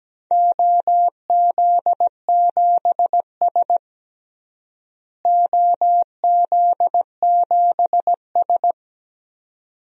Morse code, heard as a keyed tone, about 17 wpm, 700 Hz. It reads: OZ7S OZ7S
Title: DXCC-100 Call Signs - 4 Characters - Repeated 2x - 17wpm